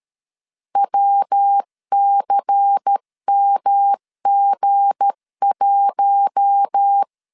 Morse code, heard as a keyed tone, about 13 wpm, 800 Hz. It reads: WCMG1
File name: LEZIONI / 7 GF9Z / 7 G 12wpm 2055